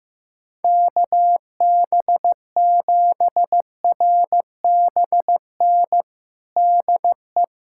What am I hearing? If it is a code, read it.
Morse code, heard as a keyed tone, about 15 wpm, 700 Hz. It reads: KB7RBN DE
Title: USA Callsign DE Callsign 15wpm